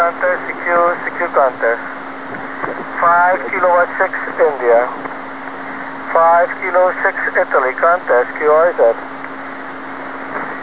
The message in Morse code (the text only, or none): none